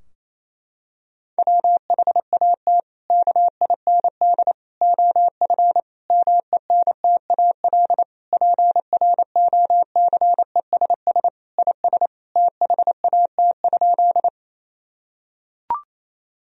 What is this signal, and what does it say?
Morse code, heard as a keyed tone, about 27 wpm, 700 Hz. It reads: WHAT KIND OF MENTAL PROCESS IS THAT?
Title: Sentences from Top 1000 Words - Review Part 4 - 28wpm